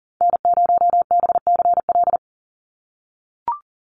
Morse code, heard as a keyed tone, about 40 wpm, 700 Hz. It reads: N0BCL
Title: USA Call Signs 40wpm